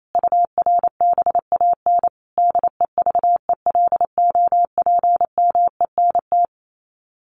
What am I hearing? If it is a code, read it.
Morse code, heard as a keyed tone, about 28 wpm, 700 Hz. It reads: URBAN DEVELOPMENT
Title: Sets of 5 Words Encore2 28wpm